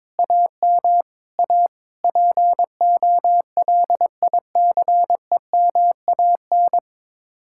Morse code, heard as a keyed tone, about 22 wpm, 700 Hz. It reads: AM A POLICEMAN
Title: Speed-Racing Common QSO Phrases - 22wpm